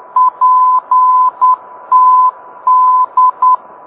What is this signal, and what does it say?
Morse code, heard as a keyed tone, about 10 wpm, 1010 Hz. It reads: PTD